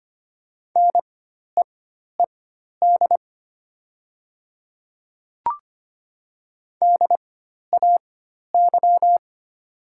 Morse code, heard as a keyed tone, about 25 wpm, spaced as for 12 wpm, 700 Hz. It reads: NEED DAY
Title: 4x Character Spacing - Top 200 Words - 25wpm